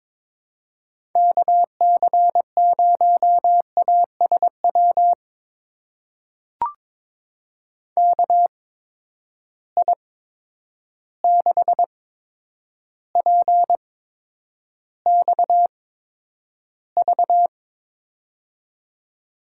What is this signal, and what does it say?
Morse code, heard as a keyed tone, about 22 wpm, spaced as for 9 wpm, 700 Hz. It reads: KC0ASW KI6PXV
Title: USA Call Signs - 6 Characters - 8x Character Spacing - 22wpm